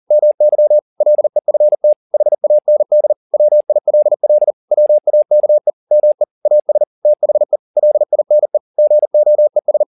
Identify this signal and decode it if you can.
Morse code, heard as a keyed tone, about 40 wpm, 600 Hz. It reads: MY LEFT HAND WILL WAKE ME AS THE LINE GOES
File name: the old man and the sea 2 40wpm